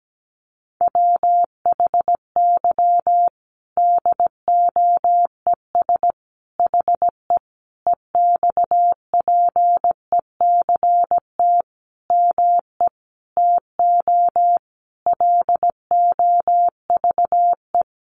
Morse code, heard as a keyed tone, about 17 wpm, 700 Hz. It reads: WHY DOES HE EXPECT ME TO LOVE
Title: Sentences from Top 1100 Words 17wpm